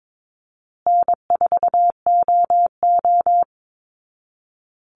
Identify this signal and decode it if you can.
Morse code, heard as a keyed tone, about 22 wpm, 700 Hz. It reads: N4OO
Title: Easy Call Signs Repeated 3x - 22wpm